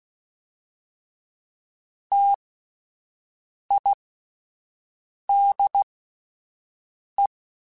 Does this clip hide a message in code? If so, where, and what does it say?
Yes: Morse code from 2.1 to 7.3 s, heard as a keyed tone, about 16 wpm, spaced as for 5 wpm, 800 Hz. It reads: TIDE